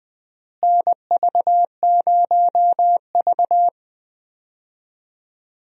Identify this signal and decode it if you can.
Morse code, heard as a keyed tone, about 20 wpm, 700 Hz. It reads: NV0V
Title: USA Call Signs - 4 Characters - Repeated 2x - 20wpm